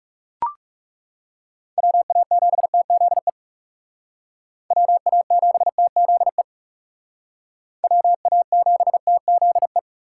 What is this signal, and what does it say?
Morse code, heard as a keyed tone, about 41 wpm, 700 Hz. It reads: WA7TZE WA7TZE WA7TZE